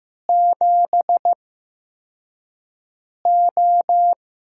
Morse code, heard as a keyed tone, about 15 wpm, spaced as for 4 wpm, 700 Hz. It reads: 7O
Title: USA Call Signs - 6 Characters - 8x Character Spacing - 15wpm